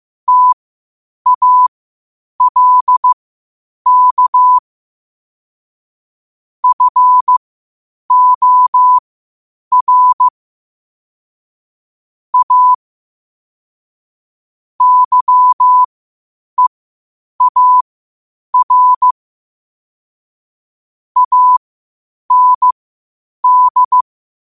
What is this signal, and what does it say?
Morse code, heard as a keyed tone, about 15 wpm, spaced as for 8 wpm, 1000 Hz. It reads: TALK FOR A YEAR AND
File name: qotdfiles / 8wpm